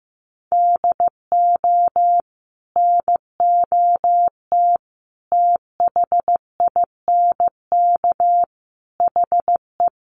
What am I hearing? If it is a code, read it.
Morse code, heard as a keyed tone, about 15 wpm, 700 Hz. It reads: DO NOT THINK HE
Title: Sets of 4 Words 15wpm